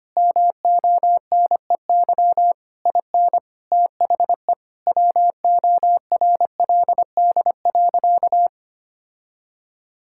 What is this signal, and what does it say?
Morse code, heard as a keyed tone, about 25 wpm, 700 Hz. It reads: MONEY IN THE WORLD.
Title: Sentences from Top 1000 Words - Review Part 6 - 25wpm